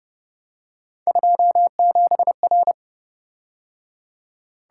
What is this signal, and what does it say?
Morse code, heard as a keyed tone, about 30 wpm, 700 Hz. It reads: 27R